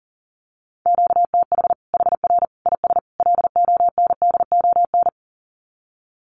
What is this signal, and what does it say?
Morse code, heard as a keyed tone, about 40 wpm, 700 Hz. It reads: QTH HR IS LONDON
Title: Common QSO Phrases 40wpm